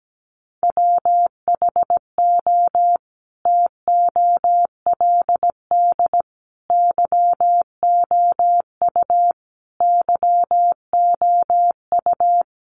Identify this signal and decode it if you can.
Morse code, heard as a keyed tone, about 17 wpm, 700 Hz. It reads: WHO TOLD YOU YOU
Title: Sentences from Top 400 Words 17wpm